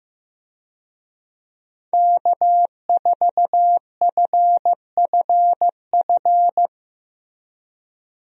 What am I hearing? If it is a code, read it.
Morse code, heard as a keyed tone, about 15 wpm, 700 Hz. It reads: K4FFF
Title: USA Call Signs - 5 Characters - Repeated 3x - 15wpm